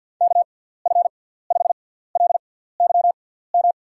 Morse code, heard as a keyed tone, about 51 wpm, spaced as for 17 wpm, 700 Hz. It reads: KF5LYM
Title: USA Call Signs - 6 Characters - 6x Character Spacing - 50wpm